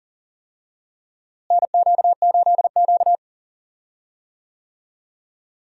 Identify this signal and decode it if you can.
Morse code, heard as a keyed tone, about 40 wpm, 700 Hz. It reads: NQ8Q
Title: USA Call Signs - 4 Characters - Repeated 3x - 40wpm